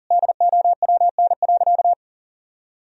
Morse code, heard as a keyed tone, about 40 wpm, 700 Hz. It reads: DOWN.
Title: Sentences from Top 1000 Words - Review Part 2 - 40wpm